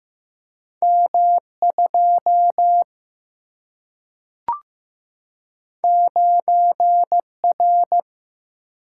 Morse code, heard as a keyed tone, about 15 wpm, 700 Hz. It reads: M2 9R
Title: Single-Letter Single-Number 15wpm